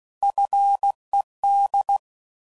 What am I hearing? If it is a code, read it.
Morse code, heard as a keyed tone, about 16 wpm, 800 Hz. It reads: FED
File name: CW Code Groups @15WPM/Technical Subjects